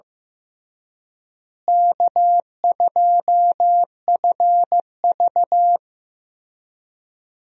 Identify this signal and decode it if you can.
Morse code, heard as a keyed tone, about 15 wpm, 700 Hz. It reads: K2FV
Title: USA Call Signs - 4 Characters - Repeated 3x - 15wpm